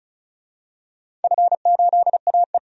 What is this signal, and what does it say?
Morse code, heard as a keyed tone, about 36 wpm, 700 Hz. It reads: F8AE
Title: DXCC-100 Call Signs - 4 Characters - 35wpm